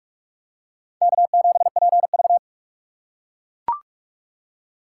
Morse code, heard as a keyed tone, about 41 wpm, 700 Hz. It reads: K7PV